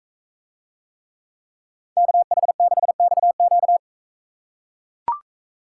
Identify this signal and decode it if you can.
Morse code, heard as a keyed tone, about 40 wpm, 700 Hz. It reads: KH6XQ